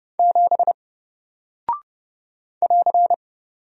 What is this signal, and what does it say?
Morse code, heard as a keyed tone, about 30 wpm, 700 Hz. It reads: 7 <AR>
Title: Single Character (Letter:Number:Punctuation:Prosign) Alt 30wpm